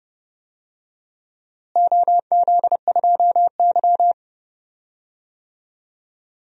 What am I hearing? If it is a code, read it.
Morse code, heard as a keyed tone, about 30 wpm, 700 Hz. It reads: OZ2Y
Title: DXCC-100 Call Signs - 4 Characters - Repeated 2x - 30wpm